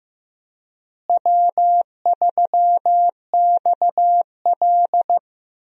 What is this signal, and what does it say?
Morse code, heard as a keyed tone, about 15 wpm, 700 Hz. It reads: W3XL
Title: USA Call Signs - 4 Characters - Repeated 3x - 15wpm